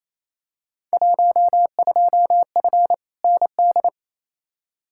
Morse code, heard as a keyed tone, about 28 wpm, 700 Hz. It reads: 12F ND
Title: Field Day Exchanges Encore - 28wpm